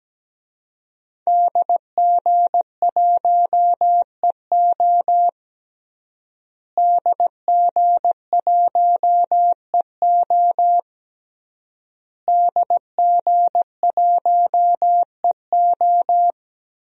Morse code, heard as a keyed tone, about 17 wpm, 700 Hz. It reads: DG1EO DG1EO DG1EO
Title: DXCC-100 Call Signs - 5 Characters - Repeated 3x - 17wpm